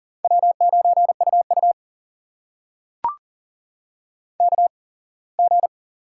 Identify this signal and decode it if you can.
Morse code, heard as a keyed tone, about 40 wpm, spaced as for 24 wpm, 690 Hz. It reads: W9UU KG
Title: USA Call Signs - 4 Characters - 8x Character Spacing - 40wpm